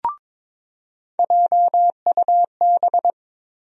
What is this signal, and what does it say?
Morse code, heard as a keyed tone, about 22 wpm, 700 Hz. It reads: JUB
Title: USA Call Sign Postfixes - 3 Letters 22wpm